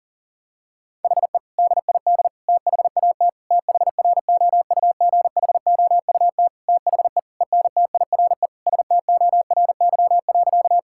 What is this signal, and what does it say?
Morse code, heard as a keyed tone, about 40 wpm, 700 Hz. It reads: HE DID THAT THROUGHOUT THE ENTIRE STORY.
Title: Sentences from Top 1000 Words - Review Part 5 - 40wpm